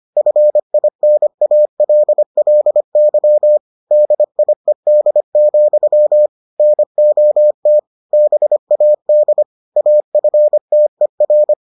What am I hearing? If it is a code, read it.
Morse code, heard as a keyed tone, about 25 wpm, 600 Hz. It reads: FINALLY DIED, NOT BAD AFTER